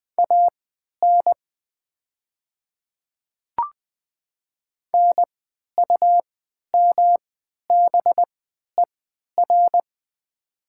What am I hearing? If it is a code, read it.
Morse code, heard as a keyed tone, about 20 wpm, spaced as for 11 wpm, 700 Hz. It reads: AN NUMBER
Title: 3x Character Spacing - Top 100 Words - 20wpm